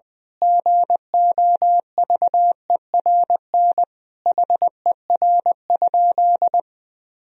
Morse code, heard as a keyed tone, about 20 wpm, 700 Hz. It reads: GOVERN HER?
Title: Sentences from Top 1000 Words - Review Part 2 - 20wpm